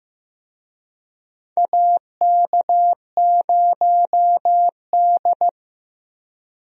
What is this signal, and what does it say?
Morse code, heard as a keyed tone, about 14 wpm, 700 Hz. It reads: AK0D
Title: USA Call Signs - 4 Characters - Repeated 3x - 15wpm